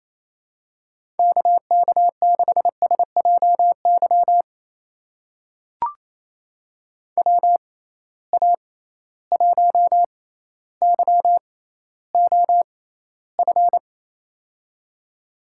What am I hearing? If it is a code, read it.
Morse code, heard as a keyed tone, about 28 wpm, spaced as for 14 wpm, 700 Hz. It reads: KK6SJY WA1YOF
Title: USA Call Signs - 6 Characters - 6x Character Spacing - 28wpm